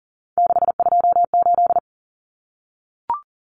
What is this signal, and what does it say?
Morse code, heard as a keyed tone, about 40 wpm, 700 Hz. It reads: 628
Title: Numbers - 3-Digit 40wpm